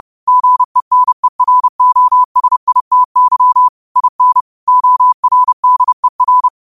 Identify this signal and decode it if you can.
Morse code, heard as a keyed tone, about 30 wpm, 1000 Hz. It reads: GENEROSITY IN ORDER
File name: qotdfiles / 30wpm